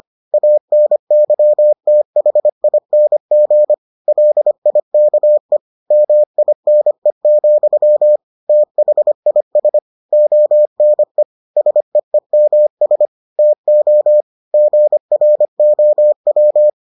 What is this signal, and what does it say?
Morse code, heard as a keyed tone, about 25 wpm, 600 Hz. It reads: ANYTHING LIKE MINE, THIS ONE SEEMS TO GROW